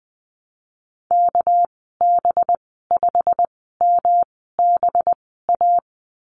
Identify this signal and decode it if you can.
Morse code, heard as a keyed tone, about 20 wpm, spaced as for 15 wpm, 700 Hz. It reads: KB5MBA
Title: 207 Lesson 33g - Letter j - Call Signs 20wpm